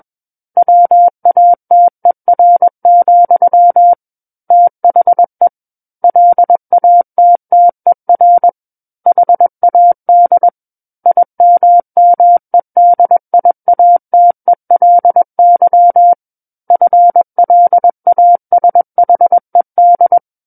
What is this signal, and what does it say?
Morse code, heard as a keyed tone, about 21 wpm, 700 Hz. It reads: WATER, THE LATTER HAD IMMEDIATELY FLASHED
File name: Book One / 12-Chapter 12 - 21 WPM